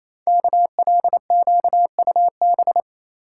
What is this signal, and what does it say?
Morse code, heard as a keyed tone, about 29 wpm, 700 Hz. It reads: KLQUB
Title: Code Groups - 5 Letter:Numbers - 28wpm